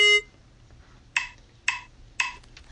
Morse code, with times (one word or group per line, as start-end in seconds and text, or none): none